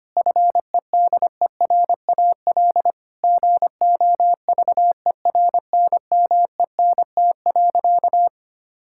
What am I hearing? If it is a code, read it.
Morse code, heard as a keyed tone, about 25 wpm, 700 Hz. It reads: FEDERAL GOVERNMENT.
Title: Sentences from Top 1000 Words - Review Part 5 - 25wpm